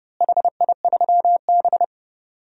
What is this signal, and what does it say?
Morse code, heard as a keyed tone, about 30 wpm, 700 Hz. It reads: HI3B
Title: DXCC-100 Call Signs - 4 Characters - Repeated 2x - 30wpm